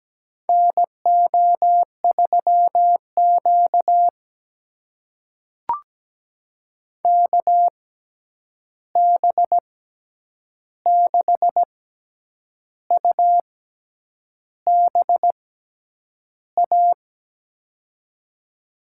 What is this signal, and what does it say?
Morse code, heard as a keyed tone, about 17 wpm, spaced as for 8 wpm, 700 Hz. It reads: NO3Q KB6UBA
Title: USA Call Signs - 6x Character Spacing - 17wpm